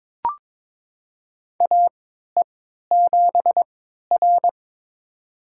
Morse code, heard as a keyed tone, about 22 wpm, spaced as for 12 wpm, 700 Hz. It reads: AE7R
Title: USA Call Signs - 3x Character Spacing - 22wpm